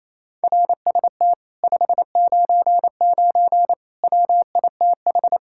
Morse code, heard as a keyed tone, about 37 wpm, 700 Hz. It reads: RST 599 WITH